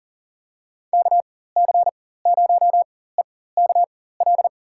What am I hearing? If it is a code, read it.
Morse code, heard as a keyed tone, about 38 wpm, spaced as for 18 wpm, 690 Hz. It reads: KC0EKL
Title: USA Call Signs - 4x Character Spacing - 40wpm